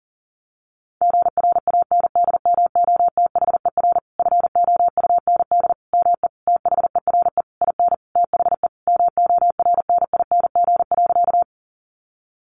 Morse code, heard as a keyed tone, about 40 wpm, 700 Hz. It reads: GRANDMOTHER FOUND ME THERE IN THE MORNING.
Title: Sentences from Top 1200 Words 40wpm